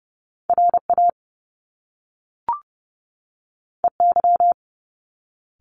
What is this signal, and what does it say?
Morse code, heard as a keyed tone, about 30 wpm, 700 Hz. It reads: RA EY
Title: Top Two Letter Combinations in Words 30wpm